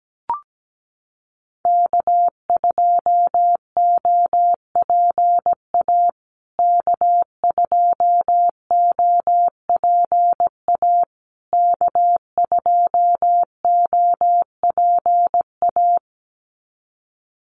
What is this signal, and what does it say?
Morse code, heard as a keyed tone, about 17 wpm, 700 Hz. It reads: K2OPA K2OPA K2OPA
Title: USA Call Signs Repeated 3x 17wpm